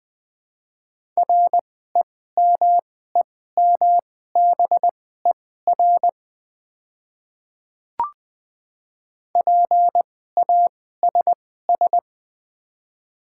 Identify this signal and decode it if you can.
Morse code, heard as a keyed tone, about 20 wpm, spaced as for 14 wpm, 700 Hz. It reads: REMEMBER PASS
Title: 2x Character Spacing - Top 500 Words - 20wpm